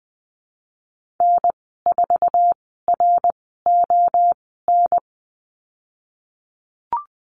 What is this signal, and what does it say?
Morse code, heard as a keyed tone, about 20 wpm, spaced as for 15 wpm, 700 Hz. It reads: N4RON
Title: 088 Lesson 15g - Number 5 - Call Signs 20wpm